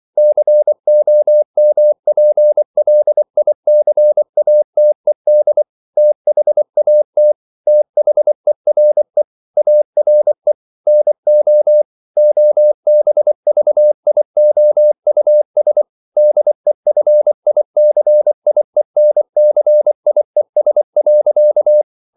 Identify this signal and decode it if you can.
Morse code, heard as a keyed tone, about 24 wpm, 600 Hz. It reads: COMPLICATED THAT THERE ARE NO OBVIOUS DEFICIENCIES.
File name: CW / Practice / EnglishQuotations0224